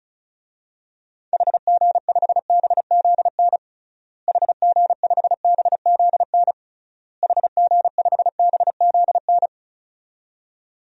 Morse code, heard as a keyed tone, about 36 wpm, 700 Hz. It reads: HG5BZN HG5BZN HG5BZN
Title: DXCC-100 Call Signs - 6 Characters - Repeated 3x - 35wpm